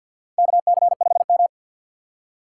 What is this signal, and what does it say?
Morse code, heard as a keyed tone, about 53 wpm, 700 Hz. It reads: KC5M
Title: USA Call Signs - 6x Character Spacing - 50wpm